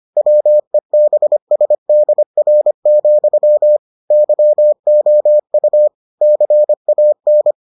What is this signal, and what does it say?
Morse code, heard as a keyed tone, about 25 wpm, 600 Hz. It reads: WEBSDR, YOU CAN